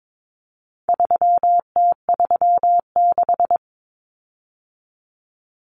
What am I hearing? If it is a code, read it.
Morse code, heard as a keyed tone, about 22 wpm, 700 Hz. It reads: 3T36
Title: Common Cut Numbers - 4-Digit 22wpm